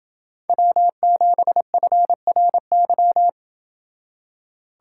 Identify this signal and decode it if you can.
Morse code, heard as a keyed tone, about 27 wpm, 700 Hz. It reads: W7FRY